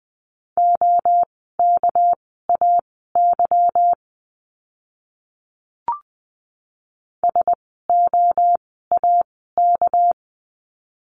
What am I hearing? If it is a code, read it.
Morse code, heard as a keyed tone, about 20 wpm, spaced as for 15 wpm, 700 Hz. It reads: OKAY SOAK
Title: 199 Lesson 32f - Letter k - Words 20wpm